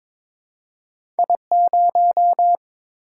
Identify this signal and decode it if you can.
Morse code, heard as a keyed tone, about 22 wpm, 700 Hz. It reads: I0
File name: practice / 22wpm / Two Letter-Number Combinations - 22wpm